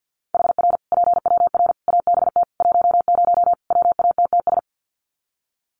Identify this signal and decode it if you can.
Morse code, heard as a keyed tone, about 50 wpm, 700 Hz. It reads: HR PWR ABT 10 WATTS
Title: Common QSO Phrases 50wpm